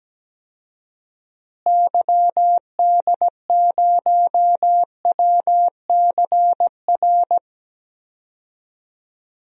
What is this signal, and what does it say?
Morse code, heard as a keyed tone, about 17 wpm, 700 Hz. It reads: YD0WCR
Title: DXCC-100 Call Signs - 6 Characters - Repeated 3x - 17wpm